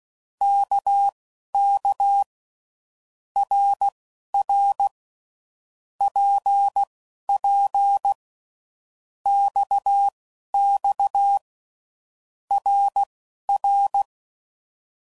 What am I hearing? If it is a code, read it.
Morse code, heard as a keyed tone, about 16 wpm, spaced as for 11 wpm, 800 Hz. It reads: KK RR PP XX RR
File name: Groups / Basic Letter Groups / RKPX